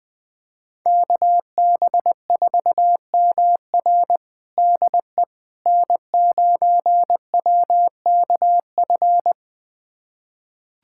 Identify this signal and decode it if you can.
Morse code, heard as a keyed tone, about 20 wpm, 700 Hz. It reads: KB4MR DE N9WKF
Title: USA Callsign DE Callsign - 5 Characters - 20wpm